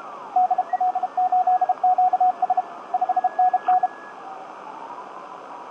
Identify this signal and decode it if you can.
Morse code, heard as a keyed tone, about 33 wpm, 750 Hz, with radio noise. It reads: DL8QS 5NN
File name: tt8tt-10m-cw-here-in-jn90ab